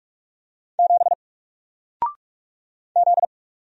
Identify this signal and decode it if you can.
Morse code, heard as a keyed tone, about 48 wpm, 700 Hz. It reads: 7 Z
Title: Single Character (Letter:Number:Punctuation:Prosign) Alt 45wpm